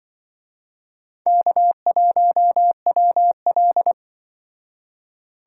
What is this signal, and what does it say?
Morse code, heard as a keyed tone, about 24 wpm, 700 Hz. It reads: K1WL
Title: Speed Racing - USA Call Signs 20wpm